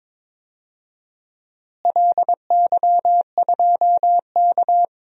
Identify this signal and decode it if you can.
Morse code, heard as a keyed tone, about 22 wpm, 700 Hz. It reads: LY2K
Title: DXCC-100 Call Signs - 4 Characters - Repeated 3x - 22wpm